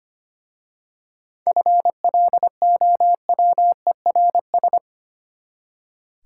Morse code, heard as a keyed tone, about 25 wpm, 700 Hz. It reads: FLOWERS
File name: practice / 25wpm / Easy Words with Hints - 25wpm